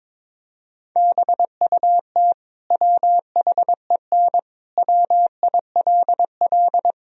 Morse code, heard as a keyed tone, about 21 wpm, 700 Hz. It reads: BUT WHEN WILL
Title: Sentences from Top 1000 Words - Review Part 3 - 22wpm